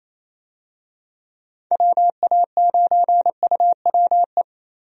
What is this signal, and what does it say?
Morse code, heard as a keyed tone, about 28 wpm, 700 Hz. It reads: WA9UWE
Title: USA Call Signs - 6 Characters - Repeated 3x - 28wpm